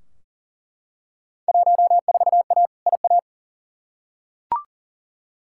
Morse code, heard as a keyed tone, about 40 wpm, 700 Hz. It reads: 14A IA
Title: Field Day Exchanges Encore - 40wpm